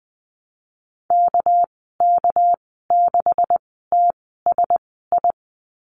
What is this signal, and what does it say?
Morse code, heard as a keyed tone, about 20 wpm, spaced as for 14 wpm, 700 Hz. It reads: KK6TSI
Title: 200 Lesson 32g - Letter k - Call Signs 20wpm